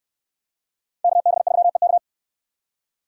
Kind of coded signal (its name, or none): Morse code